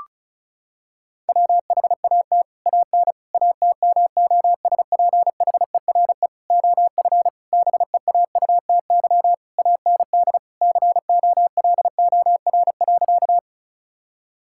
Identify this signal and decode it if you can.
Morse code, heard as a keyed tone, about 34 wpm, 700 Hz. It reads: WHAT AN ATMOSPHERE OF BEAUTY AND COLOR.
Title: Sentences from Top 1600 Words 35wpm